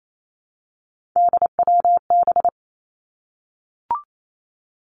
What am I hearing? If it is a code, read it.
Morse code, heard as a keyed tone, about 28 wpm, 700 Hz. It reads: DWB
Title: Three Challenging Letter-Number Combinations 28wpm